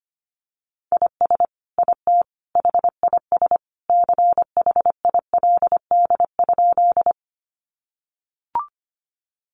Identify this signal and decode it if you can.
Morse code, heard as a keyed tone, about 25 wpm, 700 Hz. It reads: IS IT HIS CHILD?
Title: Sentences from Top 800 Words 25wpm